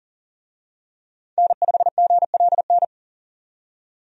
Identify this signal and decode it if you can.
Morse code, heard as a keyed tone, about 37 wpm, 700 Hz. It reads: N5GLN
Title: USA Call Signs - 5 Characters - 40wpm